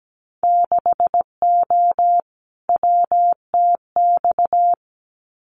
Morse code, heard as a keyed tone, about 17 wpm, 700 Hz. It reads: 6O WTX
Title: Winter Field Day Exchanges Encore 17wpm